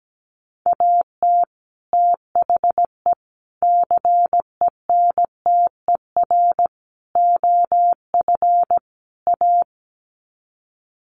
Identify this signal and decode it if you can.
Morse code, heard as a keyed tone, about 17 wpm, 700 Hz. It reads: AT THE CENTER OF A
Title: Sets of 5 Words Encore2 17wpm